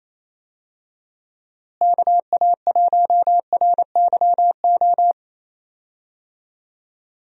Morse code, heard as a keyed tone, about 28 wpm, 700 Hz. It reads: KA1RYO
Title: USA Call Signs - 6 Characters - Repeated 3x - 28wpm